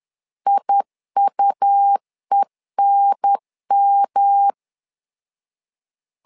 Morse code, heard as a keyed tone, about 11 wpm, 800 Hz. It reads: IUENM
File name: LEZIONI / 3 MWRO / 3 M 10wpm 1828